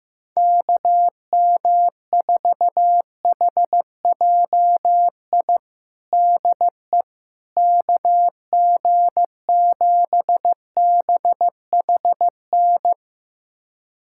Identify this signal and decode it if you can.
Morse code, heard as a keyed tone, about 15 wpm, 700 Hz. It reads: KM4HJI DE KG7BHN
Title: USA Callsign DE Callsign - 6 Characters - 15wpm